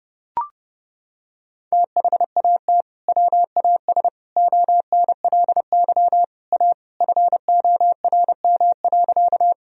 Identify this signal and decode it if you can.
Morse code, heard as a keyed tone, about 30 wpm, 700 Hz. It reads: THAT WAS ONLY A FORM.
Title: Sentences from Top 1000 Words - Review Part 1 - 30wpm